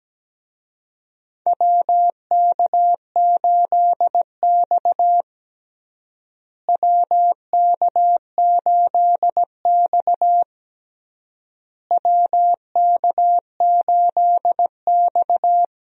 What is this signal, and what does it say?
Morse code, heard as a keyed tone, about 17 wpm, 700 Hz. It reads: WK8X WK8X WK8X
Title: USA Call Signs - 4 Characters - Repeated 3x - 17wpm